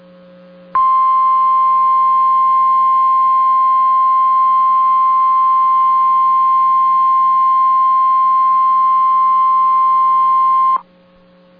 Emergency Alert System (SAME) tones (the 1050 Hz attention tone). It is heard from 0.7 to 10.8 s.